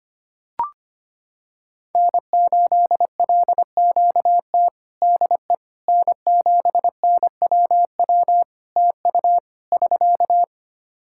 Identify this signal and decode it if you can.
Morse code, heard as a keyed tone, about 25 wpm, 700 Hz. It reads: N8LQT DE N7NWW TU <SK>
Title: Practice QSO - Exchange 4 - Ending - 25wpm